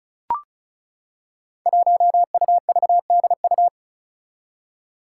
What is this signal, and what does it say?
Morse code, heard as a keyed tone, about 36 wpm, 700 Hz. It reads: 1UVDU
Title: Five Challenging Letter-Number Combinations 35wpm